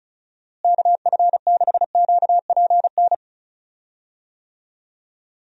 Morse code, heard as a keyed tone, about 34 wpm, 700 Hz. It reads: KF6QPN
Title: USA Call Signs - 6 Characters - Repeated 2x - 35wpm